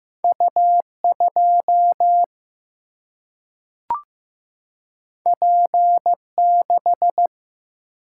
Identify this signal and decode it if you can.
Morse code, heard as a keyed tone, about 15 wpm, 700 Hz. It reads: U 2 P 6
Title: Single-Letter Single-Number 15wpm